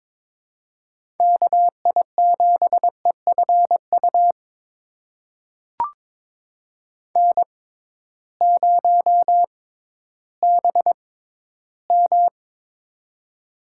Morse code, heard as a keyed tone, about 22 wpm, spaced as for 13 wpm, 700 Hz. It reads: KI7EFU N0BM
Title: USA Call Signs - 6x Character Spacing - 22wpm